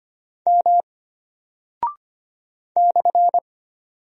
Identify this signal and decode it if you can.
Morse code, heard as a keyed tone, about 25 wpm, 700 Hz. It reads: M /
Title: Single Character (Letter:Number:Punctuation:Prosign) Alt 25wpm